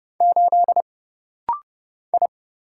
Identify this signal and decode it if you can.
Morse code, heard as a keyed tone, about 30 wpm, 700 Hz. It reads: 8 I
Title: Single Letter-Number - Rapid-Fire 30wpm